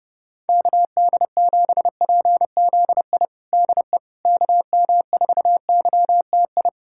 Morse code, heard as a keyed tone, about 30 wpm, 700 Hz. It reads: KD7PZI DE KM4YTI
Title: USA Callsign DE Callsign - 6 Characters - 30wpm